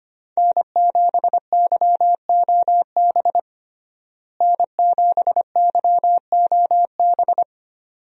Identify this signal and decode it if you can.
Morse code, heard as a keyed tone, about 25 wpm, 700 Hz. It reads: N7YOB N7YOB
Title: USA Call Signs - 5 Characters - Repeated 2x - 25wpm